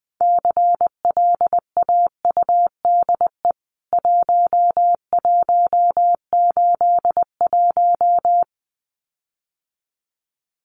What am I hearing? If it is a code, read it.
Morse code, heard as a keyed tone, about 20 wpm, 700 Hz. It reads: CLAUDE 1181
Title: CWT Exchange 20wpm